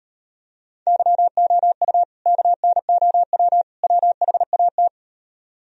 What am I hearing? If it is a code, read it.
Morse code, heard as a keyed tone, about 38 wpm, 700 Hz. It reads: YOU KNOW WHAT